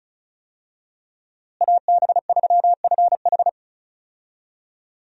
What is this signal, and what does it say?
Morse code, heard as a keyed tone, about 36 wpm, 700 Hz. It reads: AB3FH
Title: USA Call Signs - 5 Characters - Repeated 2x - 35wpm